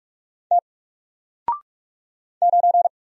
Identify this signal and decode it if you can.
Morse code, heard as a keyed tone, about 45 wpm, 700 Hz. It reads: T 9
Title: Single Character (Letter:Number:Punctuation:Prosign) Alt 45wpm